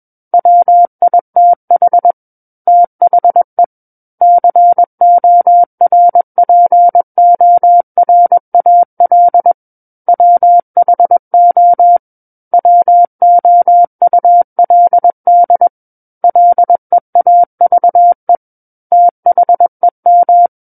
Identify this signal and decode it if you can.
Morse code, heard as a keyed tone, about 21 wpm, 700 Hz. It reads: WITH THE CORPORAL WHO WOULD LEAVE THEM
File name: Book One / 12-Chapter 12 - 21 WPM